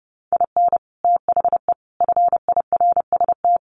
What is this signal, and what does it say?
Morse code, heard as a keyed tone, about 30 wpm, 700 Hz. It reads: IN THE FIRST
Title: Sets of 4 Words 30wpm